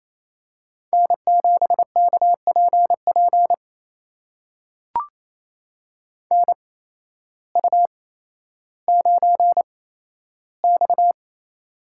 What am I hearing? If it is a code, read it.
Morse code, heard as a keyed tone, about 28 wpm, spaced as for 13 wpm, 700 Hz. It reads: N7KPP NU9X
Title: USA Call Signs - 8x Character Spacing - 28wpm